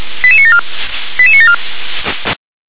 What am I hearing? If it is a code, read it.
Five-tone selective call (ZVEI), heard as quick runs of five tones, tones of about 70 ms. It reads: 88974 88974
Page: SDR/ZVEI